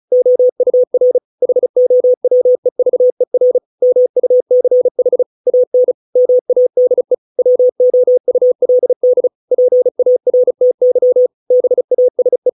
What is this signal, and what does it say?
Morse code, heard as a keyed tone, about 35 wpm, 500 Hz. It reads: OUR HOWEVER MUCH AN MADE WOULD PARTY BASE